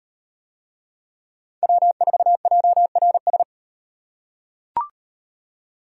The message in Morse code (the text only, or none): W4JRS